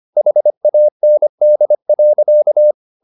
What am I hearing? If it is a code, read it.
Morse code, heard as a keyed tone, about 25 wpm, 600 Hz. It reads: HAND.
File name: the old man and the sea part 1 25wpm